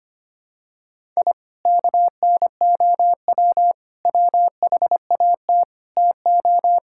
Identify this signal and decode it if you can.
Morse code, heard as a keyed tone, about 25 wpm, 700 Hz. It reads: I KNOW WHAT TO
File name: practice / 25wpm / Sentences with 3-Word Phrases - 25wpm